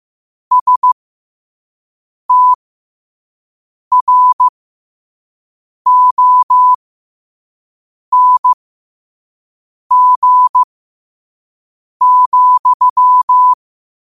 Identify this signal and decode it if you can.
Morse code, heard as a keyed tone, about 15 wpm, spaced as for 5 wpm, 1000 Hz. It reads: STRONG,
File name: qotdfiles / 5wpm